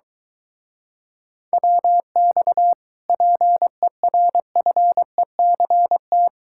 Morse code, heard as a keyed tone, about 22 wpm, 700 Hz. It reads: WX PERFECT